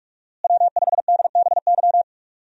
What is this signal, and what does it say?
Morse code, heard as a keyed tone, about 44 wpm, 700 Hz. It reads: W5DBY
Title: USA Call Signs - 5 Characters - Repeated 2x - 45wpm